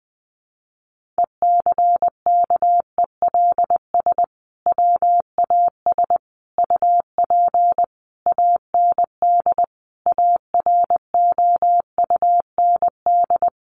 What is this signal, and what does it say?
Morse code, heard as a keyed tone, about 20 wpm, 700 Hz. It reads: ECKELS WAS UP AND AROUND
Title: A Sound Of Thunder 20wpm